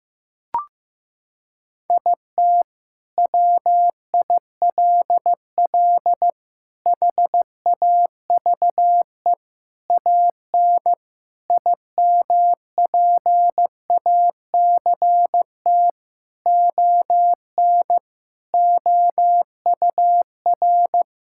Morse code, heard as a keyed tone, about 15 wpm, 700 Hz. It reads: IT WILL HAVE AN IMPACT ON OUR
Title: Sentences from Top 1000 Words - Review Part 6 - 15wpm